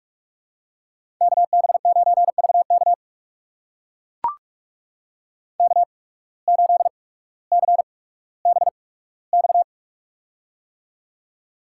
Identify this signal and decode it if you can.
Morse code, heard as a keyed tone, about 44 wpm, spaced as for 19 wpm, 700 Hz. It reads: KB9VK K8CBX
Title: USA Call Signs - 5 Characters - 8x Character Spacing - 45wpm